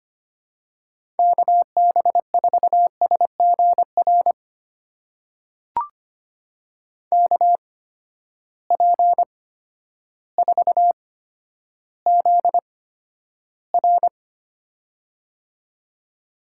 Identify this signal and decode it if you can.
Morse code, heard as a keyed tone, about 25 wpm, spaced as for 11 wpm, 700 Hz. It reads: KB4SGR KP4ZR
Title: USA Call Signs - 8x Character Spacing - 25wpm